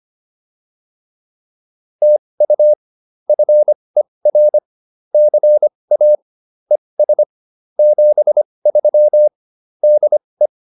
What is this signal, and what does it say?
Morse code, heard as a keyed tone, about 25 wpm, spaced as for 20 wpm, 600 Hz. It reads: TU FER CA ES 73 DE